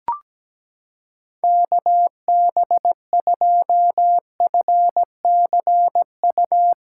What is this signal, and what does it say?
Morse code, heard as a keyed tone, about 17 wpm, 700 Hz. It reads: KB2FCU
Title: USA Callsign DE Callsign 17wpm